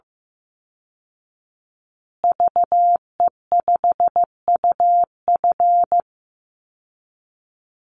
Morse code, heard as a keyed tone, about 15 wpm, 700 Hz. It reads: VE5UF
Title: Call Signs Repeated 3x - 15wpm